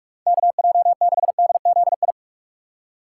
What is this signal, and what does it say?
Morse code, heard as a keyed tone, about 46 wpm, 700 Hz. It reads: KJ6DZI